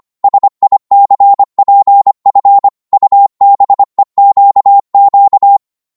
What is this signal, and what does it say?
Morse code, heard as a keyed tone, about 25 wpm, 800 Hz. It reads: SICPF UBEQQ
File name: Morse / ltr-25wpm